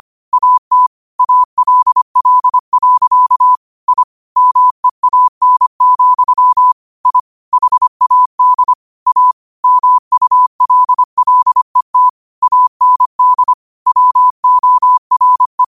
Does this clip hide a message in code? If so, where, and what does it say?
Yes: Morse code from 0.3 to 15.6 s, heard as a keyed tone, about 25 wpm, 1000 Hz. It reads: AT ALL. I MEAN, I HAD A MULLET AND WORE